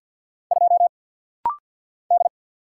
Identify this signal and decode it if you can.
Morse code, heard as a keyed tone, about 48 wpm, 700 Hz. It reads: 2 D
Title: Single Letter-Number - Rapid-Fire 50wpm